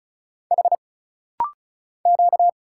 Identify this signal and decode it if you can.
Morse code, heard as a keyed tone, about 34 wpm, 700 Hz. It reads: H Q
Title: Single Letters - Rapid-Fire 35wpm